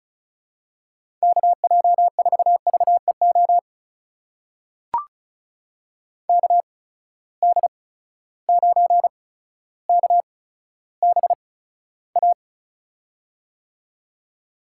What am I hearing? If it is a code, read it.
Morse code, heard as a keyed tone, about 36 wpm, spaced as for 15 wpm, 700 Hz. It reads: KJ4VEO KD9KBA
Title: USA Call Signs - 8x Character Spacing - 35wpm